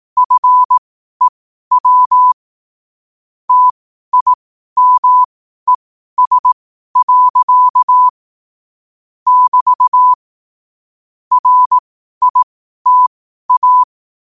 Morse code, heard as a keyed tone, about 18 wpm, spaced as for 12 wpm, 1000 Hz. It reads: FEW TIMES. <BT> RITA
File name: qotdfiles / 12wpm